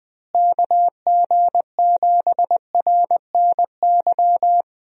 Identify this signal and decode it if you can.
Morse code, heard as a keyed tone, about 20 wpm, 700 Hz. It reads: KG7RNY
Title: USA Call Signs - 6 Characters - Repeated 3x - 20wpm